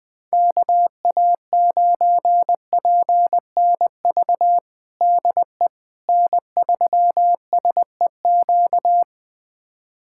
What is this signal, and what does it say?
Morse code, heard as a keyed tone, about 21 wpm, 700 Hz. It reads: KA9PNV DE N3SEQ
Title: USA Callsign DE Callsign 20wpm